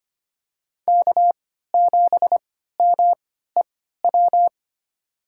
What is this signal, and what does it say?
Morse code, heard as a keyed tone, about 25 wpm, spaced as for 14 wpm, 700 Hz. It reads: K7MEW
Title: USA Call Signs - 5 Characters - 3x Character Spacing - 25wpm